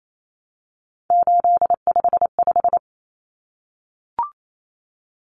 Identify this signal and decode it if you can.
Morse code, heard as a keyed tone, about 28 wpm, 700 Hz. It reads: 855
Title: Numbers - 3-Digit 28wpm